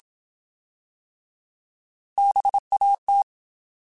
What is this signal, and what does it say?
Morse code, heard as a keyed tone, about 27 wpm, 800 Hz. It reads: BAT
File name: Alphabetical / Alphabetical mp3s / Bs @25